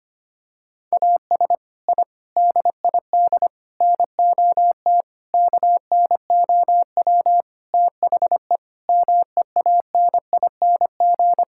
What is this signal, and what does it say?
Morse code, heard as a keyed tone, about 25 wpm, 700 Hz. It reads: AS I DID NOT KNOW THE MEANING
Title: Sentences from Top 1000 Words - Review Part 6 - 25wpm